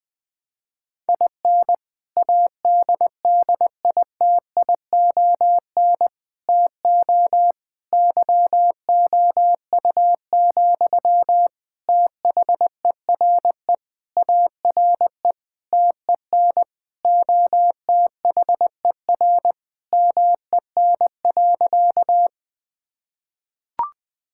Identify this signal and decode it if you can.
Morse code, heard as a keyed tone, about 20 wpm, 700 Hz. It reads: IN ADDITION TO YOU, THERE ARE TEN OTHER MEN.
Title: Sentences from Top 1000 Words - Review Part 6 - 20wpm